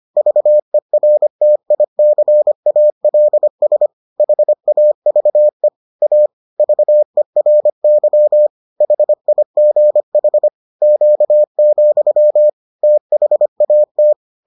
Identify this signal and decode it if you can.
Morse code, heard as a keyed tone, about 25 wpm, 600 Hz. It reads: VERTICALS HAVE A VERY HIGH Q, THAT